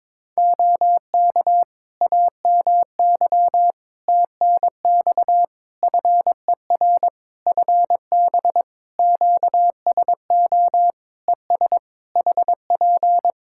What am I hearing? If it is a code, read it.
Morse code, heard as a keyed tone, about 22 wpm, 700 Hz. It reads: OK AMY TNX FER FB QSO ES HP
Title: Practice QSO - Exchange 4 - Ending - 22wpm